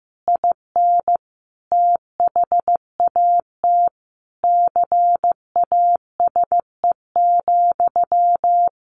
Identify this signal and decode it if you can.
Morse code, heard as a keyed tone, about 15 wpm, 700 Hz. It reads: IN THAT CASE,
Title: Sentences from Top 1400 Words 15wpm